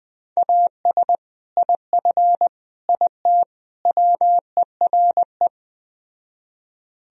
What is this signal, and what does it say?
Morse code, heard as a keyed tone, about 20 wpm, 700 Hz. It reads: AS IF IT WERE